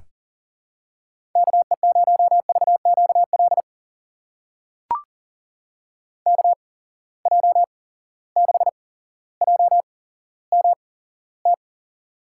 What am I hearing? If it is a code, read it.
Morse code, heard as a keyed tone, about 40 wpm, spaced as for 17 wpm, 700 Hz. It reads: KE0VQL KJ6JMT
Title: USA Call Signs - 6 Characters - 8x Character Spacing - 40wpm